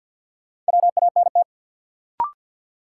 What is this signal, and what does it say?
Morse code, heard as a keyed tone, about 53 wpm, 700 Hz. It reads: WANT
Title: Sets of 2 Words - New Word Drill - 50wpm